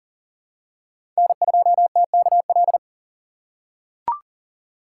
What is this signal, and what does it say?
Morse code, heard as a keyed tone, about 42 wpm, 700 Hz. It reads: N2TKL
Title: USA Call Signs - 5 Characters - 6x Character Spacing - 40wpm